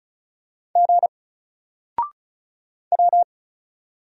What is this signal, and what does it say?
Morse code, heard as a keyed tone, about 36 wpm, 700 Hz. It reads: G W
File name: Single-Letter-Number-35wpm